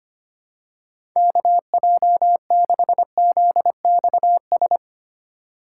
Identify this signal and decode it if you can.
Morse code, heard as a keyed tone, about 25 wpm, 700 Hz. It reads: KJ6ZXS
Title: USA Call Signs - 6 Characters - 6x Character Spacing - 25wpm